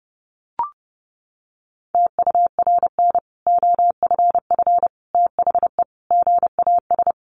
Morse code, heard as a keyed tone, about 30 wpm, 700 Hz. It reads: TURN OFF THE GAS
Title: Sentences from Top 1100 Words 30wpm